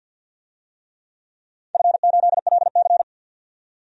Morse code, heard as a keyed tone, about 50 wpm, 700 Hz. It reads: F8LC
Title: DXCC-100 Call Signs - 4 Characters - 50wpm